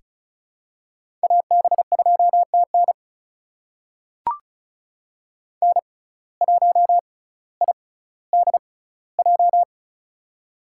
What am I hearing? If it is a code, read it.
Morse code, heard as a keyed tone, about 36 wpm, spaced as for 18 wpm, 700 Hz. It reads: AB2TN N1IDJ
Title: USA Call Signs - 5 Characters - 6x Character Spacing - 35wpm